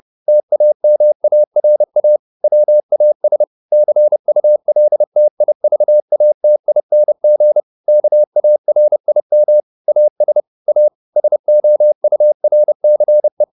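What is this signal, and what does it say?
Morse code, heard as a keyed tone, about 30 wpm, 600 Hz. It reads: TAMARA WAS CULTIVATING KARIM AS A SOURCE
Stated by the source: Morse code at 30 wpm, the novel Never by Ken Follett in English, from CW med Gnister.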